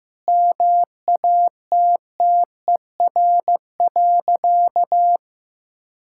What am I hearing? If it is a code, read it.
Morse code, heard as a keyed tone, about 15 wpm, 700 Hz. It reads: MATTER.
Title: Sentences from Top 1000 Words - Review Part 6 - 15wpm